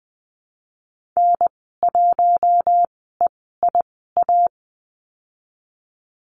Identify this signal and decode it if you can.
Morse code, heard as a keyed tone, about 20 wpm, spaced as for 14 wpm, 700 Hz. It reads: N1EIA
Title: 062 Lesson 11g - Letter h - Call Signs 20wpm